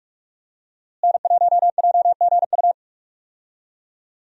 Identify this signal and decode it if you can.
Morse code, heard as a keyed tone, about 47 wpm, 700 Hz. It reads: N1JGU
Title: USA Call Signs - 5 Characters - Repeated 2x - 45wpm